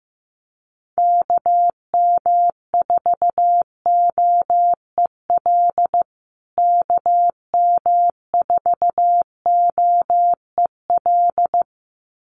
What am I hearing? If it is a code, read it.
Morse code, heard as a keyed tone, about 15 wpm, 700 Hz. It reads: KM4OEL KM4OEL
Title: USA Call Signs Repeated 2x 15wpm